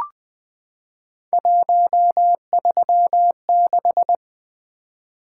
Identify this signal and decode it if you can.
Morse code, heard as a keyed tone, about 20 wpm, 700 Hz. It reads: 136
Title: Three Letter-Number Combinations - 20wpm